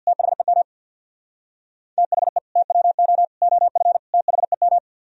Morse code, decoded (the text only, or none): THEM THE TWO OF THEM